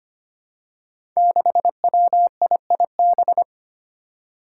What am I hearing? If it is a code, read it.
Morse code, heard as a keyed tone, about 25 wpm, 700 Hz. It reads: 6WIIB
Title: Five Challenging Letter-Number Combinations 25wpm